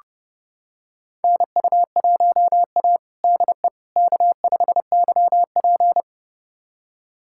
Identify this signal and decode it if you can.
Morse code, heard as a keyed tone, about 32 wpm, 700 Hz. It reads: NU1A DE K5YP
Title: USA Callsign DE Callsign - 4 Characters - 30wpm